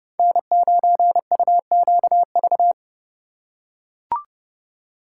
Morse code, heard as a keyed tone, about 27 wpm, 700 Hz. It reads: N9UQV